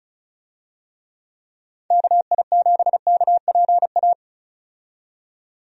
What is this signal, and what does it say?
Morse code, heard as a keyed tone, about 34 wpm, 700 Hz. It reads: KI7KPA
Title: USA Call Signs - 6 Characters - 35wpm